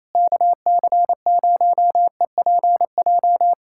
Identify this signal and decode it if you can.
Morse code, heard as a keyed tone, about 28 wpm, 700 Hz. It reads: KC0EPJ
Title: USA Callsign DE Callsign 28wpm